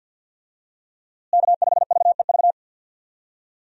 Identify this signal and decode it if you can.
Morse code, heard as a keyed tone, about 53 wpm, 700 Hz. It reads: K5VEV
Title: USA Call Signs - 5 Characters - Repeated 3x - 50wpm